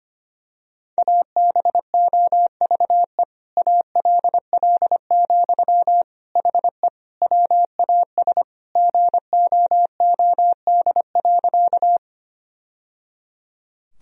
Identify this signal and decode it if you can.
Morse code, heard as a keyed tone, about 25 wpm, 700 Hz. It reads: ABOVE ALL, HE WAS GOOD.
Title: Sentences from Top 1000 Words - Review Part 2 - 25wpm